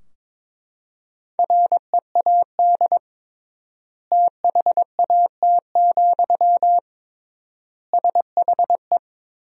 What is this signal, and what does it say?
Morse code, heard as a keyed tone, about 22 wpm, 700 Hz. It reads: READ THAT, SHE
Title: 3x Word-Spacing - Sentences from Top 200 Words - 22wpm